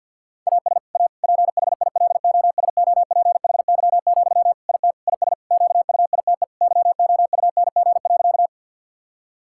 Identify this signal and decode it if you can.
Morse code, heard as a keyed tone, about 50 wpm, 700 Hz. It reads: AS A PHILOSOPHY, IT IS QUITE YOUNG.
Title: Sentences from Top 1000 Words - Review Part 6 - 50wpm